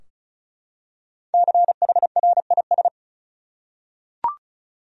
Morse code, heard as a keyed tone, about 36 wpm, 700 Hz. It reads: CHRIS
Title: USA Common Names - 35wpm